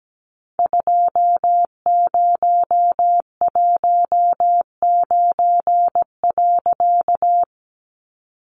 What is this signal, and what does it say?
Morse code, heard as a keyed tone, about 17 wpm, 700 Hz. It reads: 2019.
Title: A Sound Of Thunder 17wpm